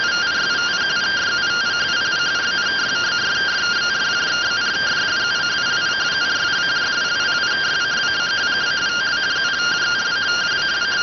Two-tone teleprinter signal, about 170 Hz shift.